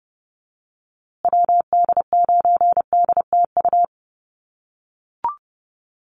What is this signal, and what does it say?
Morse code, heard as a keyed tone, about 30 wpm, 700 Hz. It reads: WD9DTU
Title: USA Call Signs Repeated 3x 30wpm